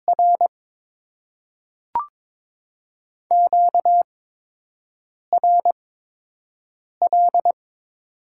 Morse code, heard as a keyed tone, about 22 wpm, spaced as for 6 wpm, 700 Hz. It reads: R QRL
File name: Top 20 QSO Elements - 8x Character Spacing - 22wpm